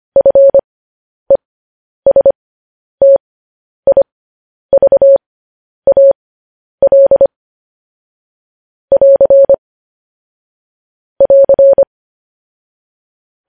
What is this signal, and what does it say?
Morse code, heard as a keyed tone, about 25 wpm, spaced as for 10 wpm, 550 Hz. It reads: FESTIVAL <AR> <AR>